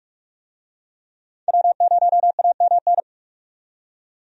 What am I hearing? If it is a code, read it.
Morse code, heard as a keyed tone, about 45 wpm, 700 Hz. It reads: W0AMN
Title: USA Call Signs - 5 Characters - 45wpm